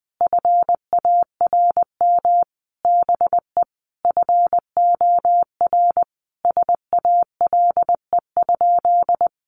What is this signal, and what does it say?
Morse code, heard as a keyed tone, about 20 wpm, 700 Hz. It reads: FARM BE FOR SALE?
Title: Sentences from Top 1400 Words 20wpm